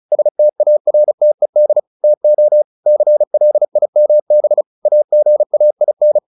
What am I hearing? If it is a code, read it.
Morse code, heard as a keyed tone, about 34 wpm, 600 Hz. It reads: STARTED TO CLIMB AGAIN
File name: the old man and the sea 2 35wpm